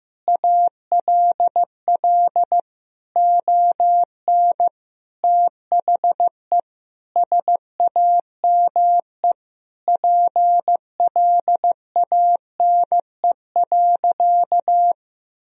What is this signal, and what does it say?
Morse code, heard as a keyed tone, about 15 wpm, 700 Hz. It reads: ALL ON THE SAME PLANE.
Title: Sentences from Top 1000 Words - Review Part 3 - 15wpm